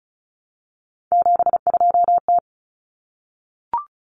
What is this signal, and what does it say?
Morse code, heard as a keyed tone, about 34 wpm, 700 Hz. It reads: 72T
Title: Common Cut Numbers - 3-Digit 35wpm